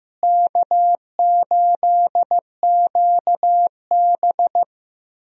Morse code, heard as a keyed tone, about 15 wpm, 700 Hz. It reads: K8QB
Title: USA Call Signs - 4 Characters - Repeated 3x - 15wpm